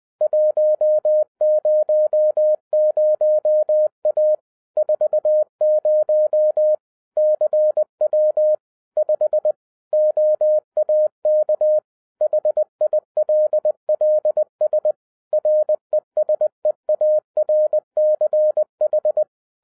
Morse code, heard as a keyed tone, about 20 wpm, 610 Hz. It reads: 100A 40 CW 5 OAK HILLS RESEARCH
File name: 2011-07-mp3 / n23 QRP OTAKU 20wpm